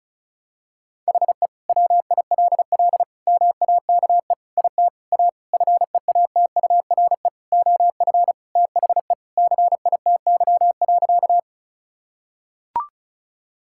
Morse code, heard as a keyed tone, about 36 wpm, 700 Hz. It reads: HE WILL MAKE IT A FEATURE OF THE CITY.
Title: Sentences from Top 1000 Words - Review Part 5 - 35wpm